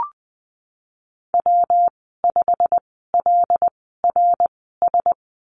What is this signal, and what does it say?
Morse code, heard as a keyed tone, about 20 wpm, spaced as for 15 wpm, 700 Hz. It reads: W5LRS
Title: 147 Lesson 24g - Letter y - Call Signs 20wpm